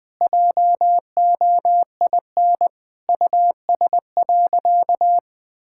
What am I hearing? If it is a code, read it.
Morse code, heard as a keyed tone, about 20 wpm, 700 Hz. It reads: JOIN US.
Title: Sentences with 6-Word Phrases - 20wpm